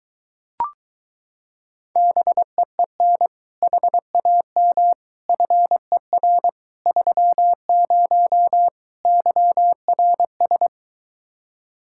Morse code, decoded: BEEN HAM FER 30 YRS

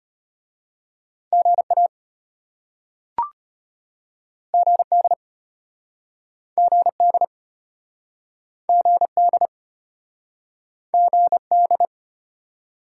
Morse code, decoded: GA GD GD GD GD

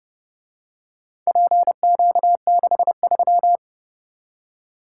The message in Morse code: PQ63

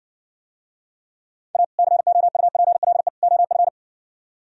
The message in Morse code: A COUPLE OF

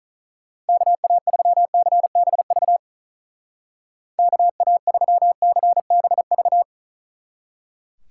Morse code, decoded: KA3CBV KA3CBV

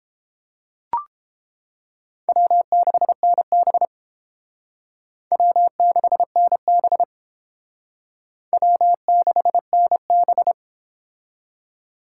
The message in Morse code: W6NB W6NB W6NB

